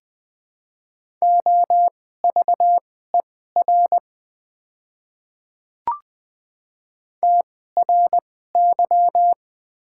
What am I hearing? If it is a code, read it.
Morse code, heard as a keyed tone, about 20 wpm, spaced as for 14 wpm, 700 Hz. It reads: OVER TRY
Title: 2x Character Spacing - Top 200 Words - 20wpm